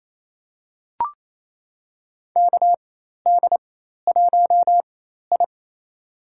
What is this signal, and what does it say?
Morse code, heard as a keyed tone, about 28 wpm, spaced as for 13 wpm, 700 Hz. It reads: KD1I
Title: USA Call Signs - 4 Characters - 4x Character Spacing - 28wpm